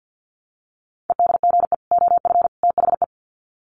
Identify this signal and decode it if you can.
Morse code, heard as a keyed tone, about 50 wpm, 700 Hz. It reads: EDGE OF THE
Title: Sets of 3 Words 50wpm